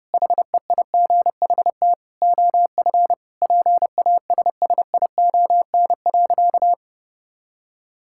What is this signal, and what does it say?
Morse code, heard as a keyed tone, about 30 wpm, 700 Hz. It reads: HEIGHT OF PASSION.
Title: Sentences from Top 1600 Words 30wpm